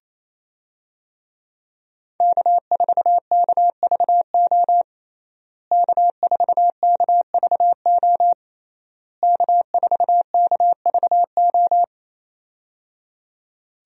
Morse code, heard as a keyed tone, about 28 wpm, 700 Hz. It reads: K4KVO K4KVO K4KVO
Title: USA Call Signs - 5 Characters - Repeated 3x - 28wpm